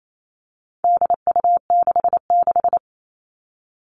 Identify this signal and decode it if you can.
Morse code, heard as a keyed tone, about 28 wpm, 700 Hz. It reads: DU66
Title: All Cut Numbers - 4-Digit 28wpm